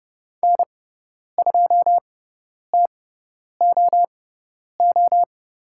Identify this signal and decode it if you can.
Morse code, heard as a keyed tone, about 30 wpm, spaced as for 10 wpm, 700 Hz. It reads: N2TOO
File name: Lesson 16g - Letter c - Call Signs 30wpm